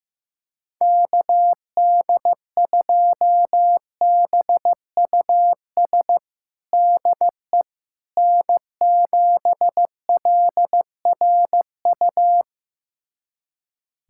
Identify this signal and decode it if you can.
Morse code, heard as a keyed tone, about 15 wpm, 700 Hz. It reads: KD2BUS DE N7LRU
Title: USA Callsign DE Callsign 15wpm